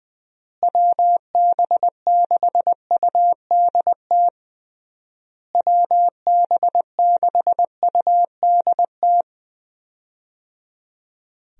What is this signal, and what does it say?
Morse code, heard as a keyed tone, about 20 wpm, 700 Hz. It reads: WB6UDT WB6UDT
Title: USA Call Signs - 6 Characters - Repeated 3x - 20wpm